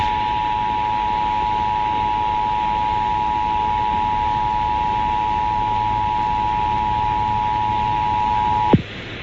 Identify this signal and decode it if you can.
Emergency Alert System (SAME) tones (the 853 and 960 Hz attention tone), with noise. It is heard from 0.4 to 8.6 s.